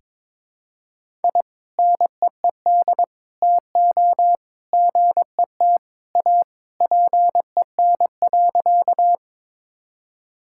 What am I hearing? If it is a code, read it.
Morse code, heard as a keyed tone, about 22 wpm, 700 Hz. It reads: I NEED TO GET A PEN.
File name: practice / 22wpm / Sentences with 2-Word Phrases - 22wpm